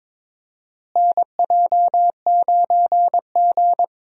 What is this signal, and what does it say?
Morse code, heard as a keyed tone, about 22 wpm, 700 Hz. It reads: NJ9G
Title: USA Call Signs - 4 Characters - Repeated 2x - 22wpm